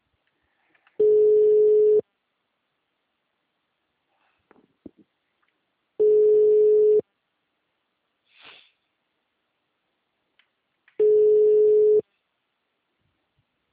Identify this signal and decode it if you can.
Ringback tone (Europe, 425 Hz).